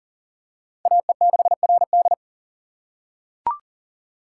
Morse code, heard as a keyed tone, about 40 wpm, 700 Hz. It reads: AE6RD
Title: USA Call Signs - 5 Characters - 6x Character Spacing - 40wpm